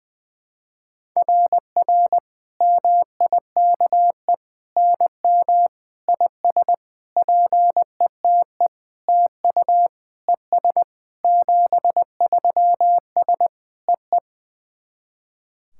Morse code, heard as a keyed tone, about 20 wpm, 690 Hz. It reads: RR MIKE NM IS PETE TU ES 73S EE
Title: SOTA QSO Practice 20wpm